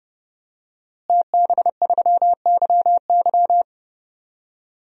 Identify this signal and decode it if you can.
Morse code, heard as a keyed tone, about 30 wpm, 700 Hz. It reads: TB3YY
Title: DXCC-100 Call Signs - 5 Characters - Repeated 2x - 30wpm